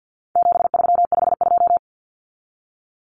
Morse code, heard as a keyed tone, about 50 wpm, 700 Hz. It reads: 7352
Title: Numbers - 4-Digit 50wpm